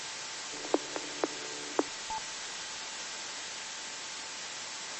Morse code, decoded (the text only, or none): none